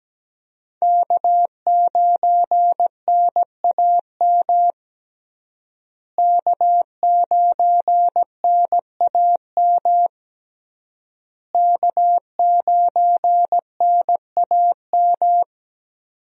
Morse code, heard as a keyed tone, about 17 wpm, 700 Hz. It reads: K9NAM K9NAM K9NAM
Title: USA Call Signs - 5 Characters - Repeated 3x - 17wpm